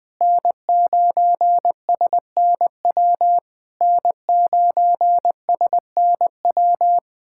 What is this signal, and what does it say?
Morse code, heard as a keyed tone, about 20 wpm, 700 Hz. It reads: N9SNW N9SNW
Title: SOTA QSO Practice 20wpm